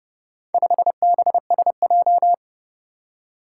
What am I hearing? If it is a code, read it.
Morse code, heard as a keyed tone, about 29 wpm, 700 Hz. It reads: 5BSJ